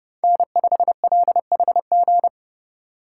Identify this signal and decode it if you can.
Morse code, heard as a keyed tone, about 30 wpm, 700 Hz. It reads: N5LHG
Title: USA Call Signs - 5 Characters - 2x Character Spacing - 30wpm